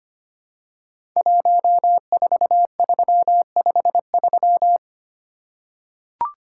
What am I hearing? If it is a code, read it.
Morse code, heard as a keyed tone, about 25 wpm, 700 Hz. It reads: 14353
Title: Code Groups - 5 Letter:Numbers - 25wpm